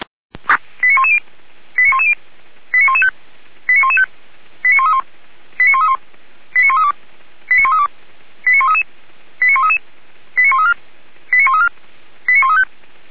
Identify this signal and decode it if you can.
Five-tone selective call (ZVEI), heard as quick runs of five tones, tones of about 70 ms. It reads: 89110 89110 89115 89115 89131 89131 89132 89132 89133 89133 89134 89134 89135